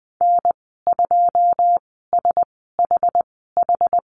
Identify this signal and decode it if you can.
Morse code, heard as a keyed tone, about 20 wpm, spaced as for 14 wpm, 700 Hz. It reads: N2SHH
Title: 088 Lesson 15g - Number 5 - Call Signs 20wpm